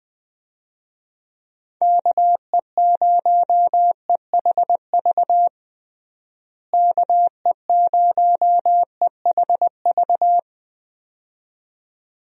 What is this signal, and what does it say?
Morse code, heard as a keyed tone, about 21 wpm, 700 Hz. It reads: KE0EHV KE0EHV
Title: USA Call Signs - 6 Characters - Repeated 2x - 20wpm